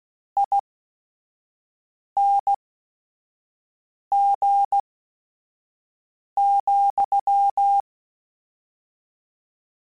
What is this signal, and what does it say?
Morse code, heard as a keyed tone, about 16 wpm, spaced as for 5 wpm, 800 Hz. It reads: ING,
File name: Suffixes / General / ESE INGs